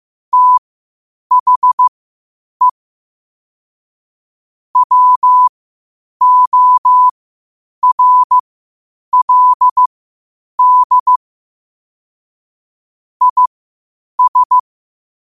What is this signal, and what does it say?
Morse code, heard as a keyed tone, about 15 wpm, spaced as for 8 wpm, 1000 Hz. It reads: THE WORLD IS
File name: qotdfiles / 8wpm